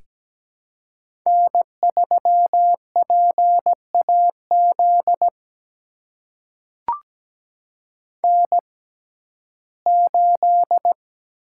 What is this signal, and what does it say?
Morse code, heard as a keyed tone, about 17 wpm, spaced as for 12 wpm, 700 Hz. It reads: N3PAZ N8
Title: USA Call Signs - 5 Characters - 6x Character Spacing - 17wpm